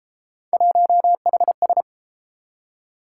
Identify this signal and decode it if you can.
Morse code, heard as a keyed tone, about 33 wpm, 700 Hz. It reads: 1HS